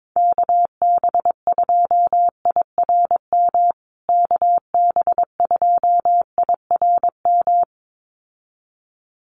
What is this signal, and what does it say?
Morse code, heard as a keyed tone, about 22 wpm, 700 Hz. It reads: KB2IRM KB2IRM
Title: USA Call Signs Repeated 2x 22wpm